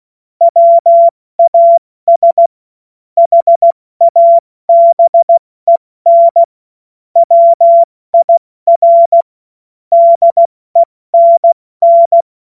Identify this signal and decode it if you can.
Morse code, heard as a keyed tone, about 16 wpm, 680 Hz. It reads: WAS HABEN WIR DENN
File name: Morsekurs / Witze / Witz 14.November 2023 70 F80